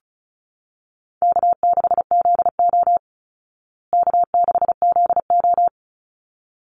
Morse code, heard as a keyed tone, about 34 wpm, 700 Hz. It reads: K6ZO K6ZO
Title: Easy Call Signs Repeated 2x - 35wpm